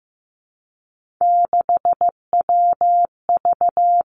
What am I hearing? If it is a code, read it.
Morse code, heard as a keyed tone, about 15 wpm, 700 Hz. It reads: 6WV
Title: Three Challenging Letter-Number Combinations 15wpm